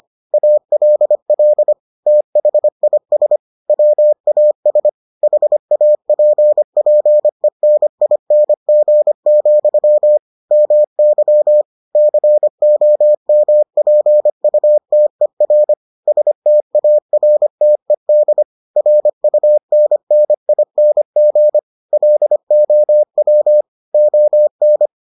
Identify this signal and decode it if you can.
Morse code, heard as a keyed tone, about 25 wpm, 600 Hz. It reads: ALL THIS WAS HAPPENING, MY COMPUTER STARTED RUNNING LOW ON